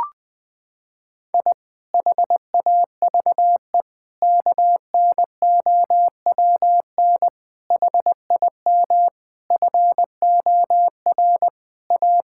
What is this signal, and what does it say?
Morse code, heard as a keyed tone, about 20 wpm, 700 Hz. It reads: I HAVE KNOWN HIM FOR A
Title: Sentences with 4-Word Phrases - 20wpm